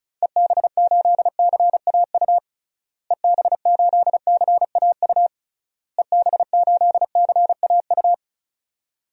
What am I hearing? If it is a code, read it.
Morse code, heard as a keyed tone, about 34 wpm, 700 Hz. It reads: EB8CAU EB8CAU EB8CAU
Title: DXCC-100 Call Signs - 6 Characters - Repeated 3x - 35wpm